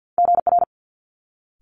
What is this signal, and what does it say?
Morse code, heard as a keyed tone, about 50 wpm, 700 Hz. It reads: GR
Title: Multi-letter Phonemes Quick Fire 50wpm